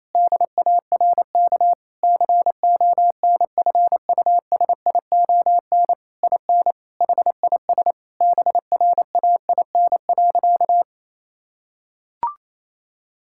Morse code, heard as a keyed tone, about 28 wpm, 700 Hz. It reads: DARK CONFUSION IN HIS BRAIN.
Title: Sentences from Top 1600 Words 28wpm